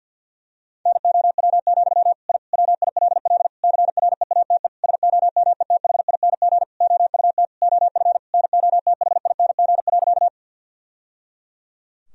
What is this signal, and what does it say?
Morse code, heard as a keyed tone, about 53 wpm, 700 Hz. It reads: NOW, I WILL CREATE SOMETHING OUT OF NOTHING.
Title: Sentences from Top 1000 Words - Review Part 3 - 50wpm